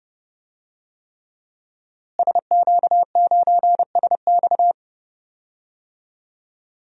Morse code, heard as a keyed tone, about 30 wpm, 700 Hz. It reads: SQ9SX
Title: DXCC-100 Call Signs - 5 Characters - Repeated 3x - 30wpm